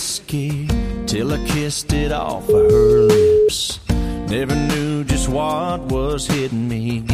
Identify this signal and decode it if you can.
Ringback tone (Europe, 425 Hz), with noise.